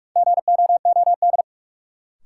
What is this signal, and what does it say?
Morse code, heard as a keyed tone, about 46 wpm, 700 Hz. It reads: GOOD
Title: Sentences from Top 200 Words - New Word Drill - 45wpm